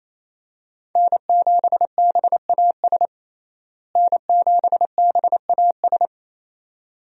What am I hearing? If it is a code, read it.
Morse code, heard as a keyed tone, about 27 wpm, 700 Hz. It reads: N7BAS N7BAS
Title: USA Call Signs - 5 Characters - Repeated 2x - 28wpm